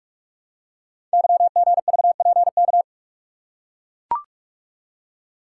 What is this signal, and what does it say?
Morse code, heard as a keyed tone, about 44 wpm, 700 Hz. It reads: YGVPK
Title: Code Groups - 5 Letters - 45wpm